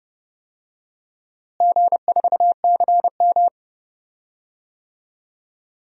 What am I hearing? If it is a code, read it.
Morse code, heard as a keyed tone, about 32 wpm, 700 Hz. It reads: G4CM
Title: DXCC-100 Call Signs - 4 Characters - Repeated 3x - 30wpm